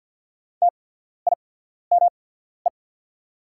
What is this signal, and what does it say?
Morse code, heard as a keyed tone, about 48 wpm, spaced as for 14 wpm, 690 Hz. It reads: TIME
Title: 8x Character Spacing - Top 200 Words - 50wpm